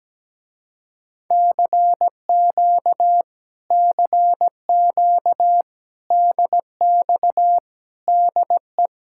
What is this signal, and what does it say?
Morse code, heard as a keyed tone, about 17 wpm, 700 Hz. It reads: CQ CQ DX DE
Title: Speed-Racing Common QSO Phrases - 17wpm